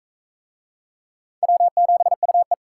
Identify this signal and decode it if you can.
Morse code, heard as a keyed tone, about 40 wpm, 700 Hz. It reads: W7UE